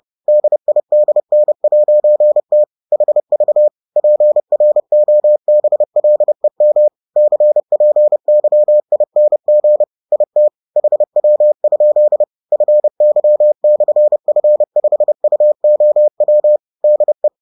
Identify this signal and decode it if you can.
Morse code, heard as a keyed tone, about 30 wpm, 600 Hz. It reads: DIDN'T HV PROBLEM CPYING IT HW? FY/F5UOW DE